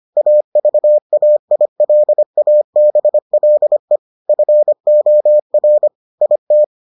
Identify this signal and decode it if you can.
Morse code, heard as a keyed tone, about 25 wpm, 600 Hz. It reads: AVAILABLE FOR IT